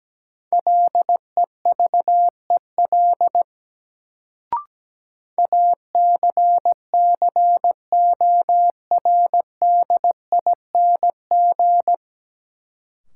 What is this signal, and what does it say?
Morse code, heard as a keyed tone, about 17 wpm, 700 Hz. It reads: LEVEL ACCORDING
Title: Sentences from Top 700 Words - New Word Drill - 17wpm